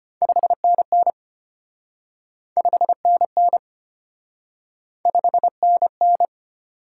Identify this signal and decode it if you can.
Morse code, heard as a keyed tone, about 33 wpm, 700 Hz. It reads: 5NN 5NN 5NN